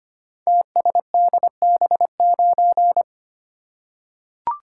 Morse code, heard as a keyed tone, about 25 wpm, 700 Hz. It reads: TSDB9
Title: Code Groups - 5 Letter:Numbers - 25wpm